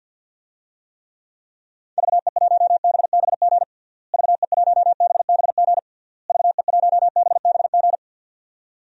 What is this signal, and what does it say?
Morse code, heard as a keyed tone, about 51 wpm, 700 Hz. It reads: VE1BBG VE1BBG VE1BBG
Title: DXCC-100 Call Signs - 6 Characters - Repeated 3x - 50wpm